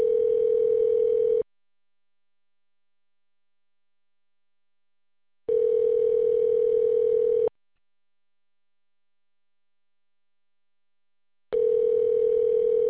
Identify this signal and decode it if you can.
Ringback tone (North America), 440 + 480 Hz.